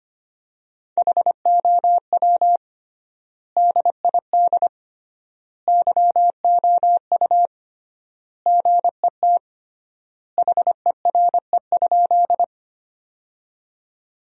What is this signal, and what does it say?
Morse code, heard as a keyed tone, about 25 wpm, 700 Hz. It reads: HOW DID YOU GET HERE?
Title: 3x Word-Spacing - Sentences from Top 200 Words - 25wpm